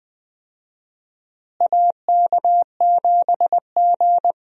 Morse code, heard as a keyed tone, about 20 wpm, 700 Hz. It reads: AK7G
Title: USA Call Signs - 4 Characters - Repeated 2x - 20wpm